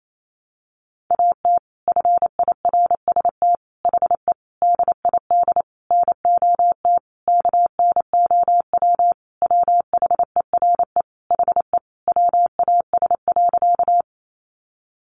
Morse code, heard as a keyed tone, about 28 wpm, 700 Hz. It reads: AT FIRST HE DID NOT KNOW WHERE HE WAS.
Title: Sentences from Top 600 Words 28wpm